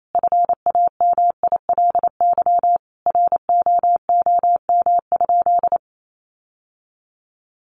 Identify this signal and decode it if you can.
Morse code, heard as a keyed tone, about 28 wpm, 700 Hz. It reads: FAMILY ROOM?
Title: Sentences from Top 1200 Words 28wpm